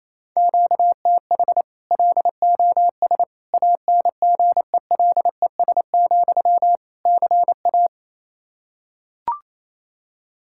Morse code, heard as a keyed tone, about 28 wpm, 700 Hz. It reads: QTH LOS ANGELES, CA
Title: Practice QSO - Exchange 1 - Beginning - 28wpm